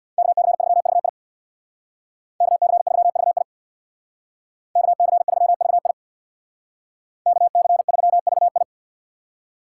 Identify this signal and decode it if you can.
Morse code, heard as a keyed tone, about 62 wpm, 700 Hz. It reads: KC3VI KC3VI KC3VI KC3VI
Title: Speed Racing - USA Call Signs 50wpm